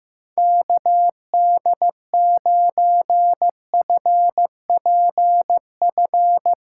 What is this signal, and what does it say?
Morse code, heard as a keyed tone, about 16 wpm, 700 Hz. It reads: KD9FPF
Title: USA Callsign DE Callsign - 6 Characters - 15wpm